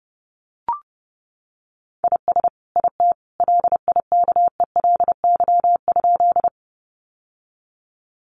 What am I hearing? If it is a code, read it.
Morse code, heard as a keyed tone, about 30 wpm, 700 Hz. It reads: IS IT LIKELY?
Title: Sentences from Top 600 Words 30wpm